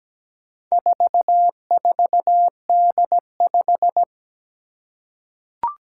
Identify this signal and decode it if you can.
Morse code, heard as a keyed tone, about 17 wpm, 700 Hz. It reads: 44D5
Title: Four Challenging Letter-Number Combinations 17wpm